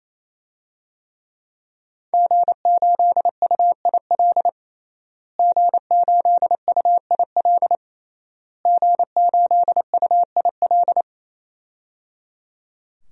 Morse code, heard as a keyed tone, about 27 wpm, 700 Hz. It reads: G8UIL G8UIL G8UIL
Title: DXCC-100 Call Signs - 5 Characters - Repeated 3x - 28wpm